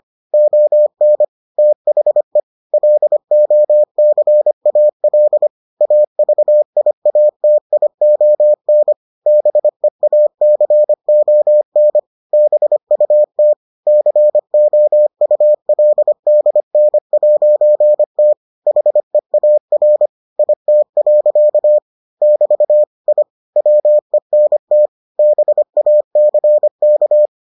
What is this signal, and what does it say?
Morse code, heard as a keyed tone, about 25 wpm, 600 Hz. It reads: ON THE LOCAL AVIATION BEACON BUT COULDN'T HEAR IT. <BT> I WENT BACK